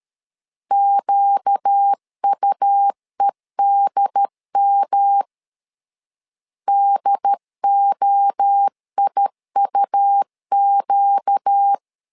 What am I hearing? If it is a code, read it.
Morse code, heard as a keyed tone, about 13 wpm, 800 Hz. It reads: QUEDM DOIUQ
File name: LEZIONI / 4 Q 12wpm 1833